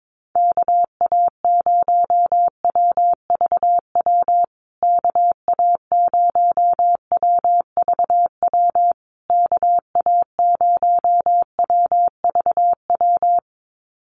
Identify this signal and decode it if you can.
Morse code, heard as a keyed tone, about 22 wpm, 700 Hz. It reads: KA0WVW KA0WVW KA0WVW
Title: USA Call Signs Repeated 3x 22wpm